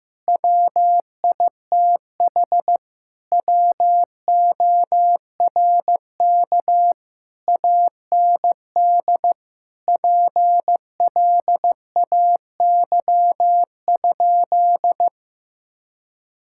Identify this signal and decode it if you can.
Morse code, heard as a keyed tone, about 15 wpm, 700 Hz. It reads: WITH WORK AND PLAY?
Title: Sentences from Top 1000 Words - Review Part 6 - 15wpm